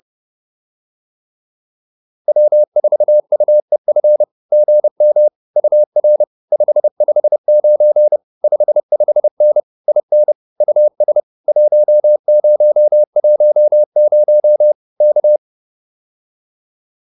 Morse code, heard as a keyed tone, about 30 wpm, 590 Hz. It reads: W4UEF GM UR 559 55N IN US 1010 K